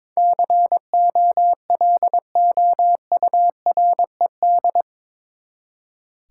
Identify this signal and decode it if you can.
Morse code, heard as a keyed tone, about 22 wpm, 700 Hz. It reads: COLOURED
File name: practice / 22wpm / Eight Letter Words - 22wpm